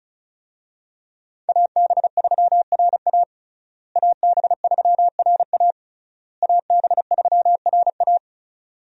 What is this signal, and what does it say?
Morse code, heard as a keyed tone, about 36 wpm, 700 Hz. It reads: AB3RA AB3RA AB3RA
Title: USA Call Signs - 5 Characters - Repeated 3x - 35wpm